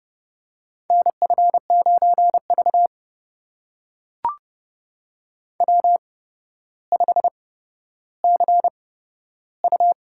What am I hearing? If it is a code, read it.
Morse code, heard as a keyed tone, about 30 wpm, spaced as for 13 wpm, 700 Hz. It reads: NF9V W5CU
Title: USA Call Signs - 4 Characters - 8x Character Spacing - 30wpm